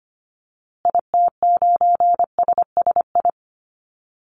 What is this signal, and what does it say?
Morse code, heard as a keyed tone, about 25 wpm, 700 Hz. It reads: IT9SSI
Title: Call Signs Repeated 3x - 25wpm